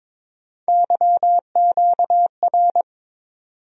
Morse code, heard as a keyed tone, about 21 wpm, 700 Hz. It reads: YQR
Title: USA Call Sign Postfixes - 3 Letters 22wpm